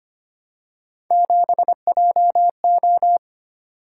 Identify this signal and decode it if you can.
Morse code, heard as a keyed tone, about 25 wpm, 700 Hz. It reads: 7JO
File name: Three Letter-Number Combinations - 25wpm